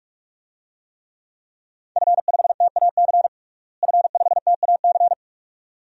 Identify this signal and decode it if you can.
Morse code, heard as a keyed tone, about 46 wpm, 700 Hz. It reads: F5TAC F5TAC
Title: DXCC-100 Call Signs - 5 Characters - Repeated 2x - 45wpm